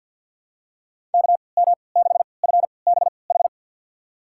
Morse code, heard as a keyed tone, about 51 wpm, spaced as for 28 wpm, 700 Hz. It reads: KM6FBH
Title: USA Call Signs - 6 Characters - 3x Character Spacing - 50wpm